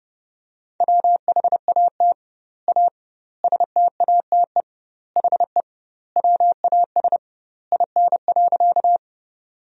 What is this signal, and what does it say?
Morse code, heard as a keyed tone, about 32 wpm, 700 Hz. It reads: WHAT A STATE HE WAS IN.
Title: 2x Word-Spacing - Sentences from Top 400 Words - 30wpm